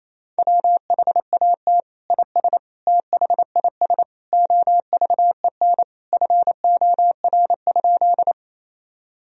Morse code, heard as a keyed tone, about 28 wpm, 700 Hz. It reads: WHAT IS THIS OVEN FOR?
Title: Sentences from Top 1000 Words - Review Part 4 - 28wpm